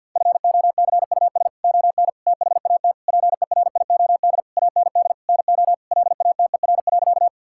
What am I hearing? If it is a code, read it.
Morse code, heard as a keyed tone, about 46 wpm, 690 Hz. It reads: FOCUS ON THAT PERIOD AND NO LATER.
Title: Sentences from Top 1000 Words - Review Part 5 - 50wpm